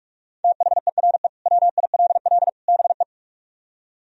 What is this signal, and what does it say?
Morse code, heard as a keyed tone, about 44 wpm, 700 Hz. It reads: THERE WILL BE